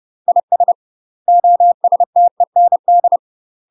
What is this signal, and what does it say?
Morse code, heard as a keyed tone, about 30 wpm, 700 Hz. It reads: IS OSTEND
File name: UK MASTER QSOs TRAINING / 30WPM / UK MASTER QSOs 0045